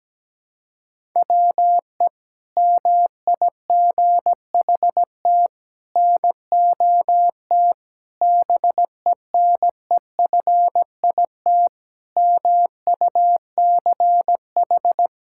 Morse code, heard as a keyed tone, about 17 wpm, 700 Hz. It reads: WE MIGHT NOT BENEFIT MUCH
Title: Sentences from Top 1000 Words - Review Part 6 - 17wpm